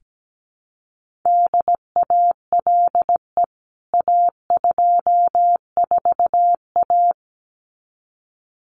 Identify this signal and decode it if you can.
Morse code, heard as a keyed tone, about 17 wpm, 700 Hz. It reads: DALE A24A
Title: CWT Exchange 17wpm